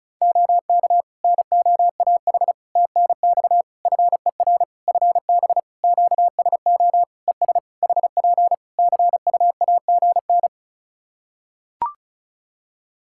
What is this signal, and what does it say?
Morse code, heard as a keyed tone, about 34 wpm, 700 Hz. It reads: OK NOAH TNX FER FB QSO ES HP CUAGN
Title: Practice QSO - Exchange 4 - Ending - 35wpm